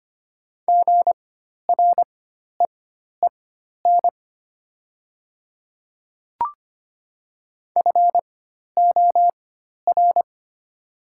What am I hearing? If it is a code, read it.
Morse code, heard as a keyed tone, about 25 wpm, spaced as for 12 wpm, 700 Hz. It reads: GREEN FOR
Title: 4x Character Spacing - Top 500 Words - 25wpm